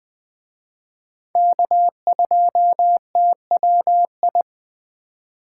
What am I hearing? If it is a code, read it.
Morse code, heard as a keyed tone, about 19 wpm, 700 Hz. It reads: K2TWI